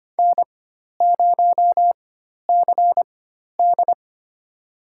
Morse code, heard as a keyed tone, about 25 wpm, spaced as for 12 wpm, 700 Hz. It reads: N0CD
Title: USA Call Signs - 4 Characters - 4x Character Spacing - 25wpm